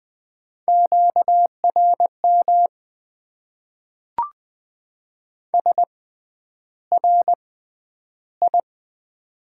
Morse code, heard as a keyed tone, about 20 wpm, spaced as for 10 wpm, 700 Hz. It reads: QRM SRI
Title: Top 90 QSO Elements - 6x Character Spacing - 20wpm